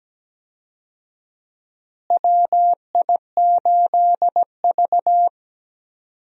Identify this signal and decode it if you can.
Morse code, heard as a keyed tone, about 17 wpm, 700 Hz. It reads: WI8V
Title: USA Call Signs - 4 Characters - Repeated 3x - 17wpm